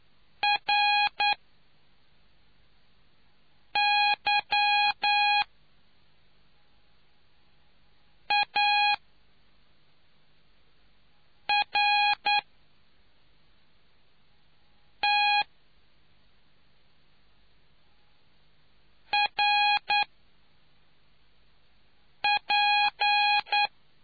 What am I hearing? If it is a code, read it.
Morse code, heard as a keyed tone, about 9 wpm, 820 Hz. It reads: R Y A R T R P